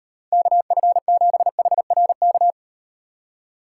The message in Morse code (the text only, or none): KF7HRK